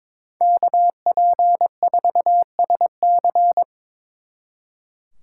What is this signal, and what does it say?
Morse code, heard as a keyed tone, about 26 wpm, 700 Hz. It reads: KP4SC
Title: Speed Racing - USA 2x2 Call Signs 22wpm